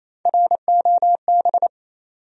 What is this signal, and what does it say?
Morse code, heard as a keyed tone, about 28 wpm, 700 Hz. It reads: ROB
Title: USA Common Names - Max 3 Letters - 28wpm